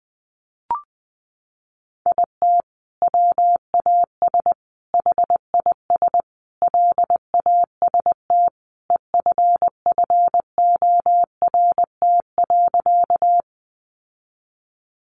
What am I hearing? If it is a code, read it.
Morse code, heard as a keyed tone, about 20 wpm, 700 Hz. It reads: IT WAS HIS LAST EFFORT.
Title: Sentences from Top 1200 Words 20wpm